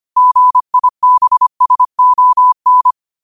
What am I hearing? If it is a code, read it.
Morse code, heard as a keyed tone, about 25 wpm, 1000 Hz. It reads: GIBSON